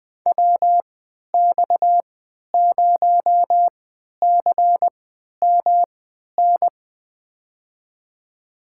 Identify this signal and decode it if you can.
Morse code, heard as a keyed tone, about 20 wpm, spaced as for 11 wpm, 700 Hz. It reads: WX0CMN
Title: USA Call Signs - 6 Characters - 3x Character Spacing - 20wpm